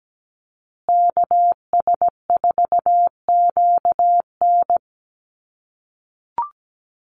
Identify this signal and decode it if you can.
Morse code, heard as a keyed tone, about 17 wpm, 700 Hz. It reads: KS4QN
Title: USA Call Signs Repeated 3x 17wpm